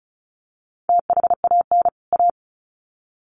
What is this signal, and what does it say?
Morse code, heard as a keyed tone, about 34 wpm, 700 Hz. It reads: THAN A
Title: Sets of 2 Words 35wpm